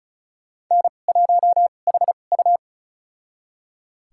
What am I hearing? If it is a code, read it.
Morse code, heard as a keyed tone, about 34 wpm, spaced as for 25 wpm, 690 Hz. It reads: N1HU
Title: USA Call Signs - 4 Characters - 2x Character Spacing - 35wpm